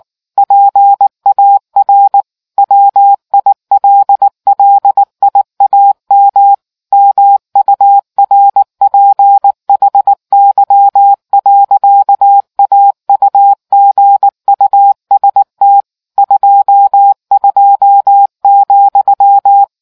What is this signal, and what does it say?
Morse code, heard as a keyed tone, about 19 wpm, 800 Hz. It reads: PAR WILLIAM MURPHY. AUGUST 22,